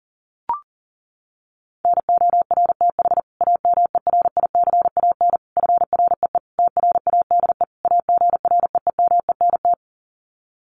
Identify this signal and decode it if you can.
Morse code, heard as a keyed tone, about 40 wpm, 700 Hz. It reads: NORTH AMERICAN FREE TRADE AGREEMENT
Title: Sets of 5 Words Encore 40wpm